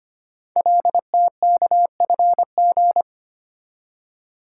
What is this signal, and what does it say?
Morse code, heard as a keyed tone, about 25 wpm, 700 Hz. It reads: LTKFG
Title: Code Groups - 5 Letters - 25wpm